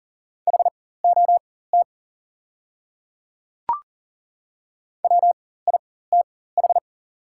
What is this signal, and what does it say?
Morse code, heard as a keyed tone, about 40 wpm, spaced as for 19 wpm, 690 Hz. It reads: HOT WITH
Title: 4x Character Spacing - Top 500 Words - 40wpm